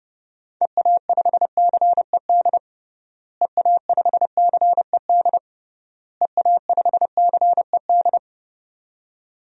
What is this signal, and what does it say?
Morse code, heard as a keyed tone, about 30 wpm, 700 Hz. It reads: EA5CED EA5CED EA5CED
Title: DXCC-100 Call Signs - 6 Characters - Repeated 3x - 30wpm